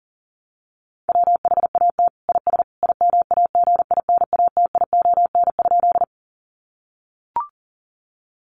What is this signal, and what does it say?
Morse code, heard as a keyed tone, about 40 wpm, 700 Hz. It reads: WHAT IS IMAGINATION?
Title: Sentences from Top 1300 Words 40wpm